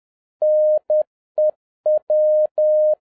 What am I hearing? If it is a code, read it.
Morse code, heard as a keyed tone, about 10 wpm, 610 Hz. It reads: NEW